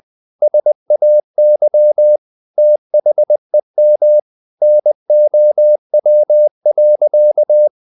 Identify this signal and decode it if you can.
Morse code, heard as a keyed tone, about 20 wpm, 600 Hz. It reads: SAY THEM NOW.
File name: the old man and the sea part 2 20wpm